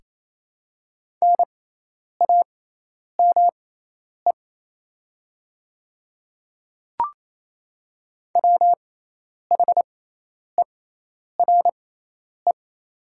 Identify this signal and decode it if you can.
Morse code, heard as a keyed tone, about 28 wpm, spaced as for 10 wpm, 700 Hz. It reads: NAME WHERE
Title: 6x Character Spacing - Top 200 Words - 28wpm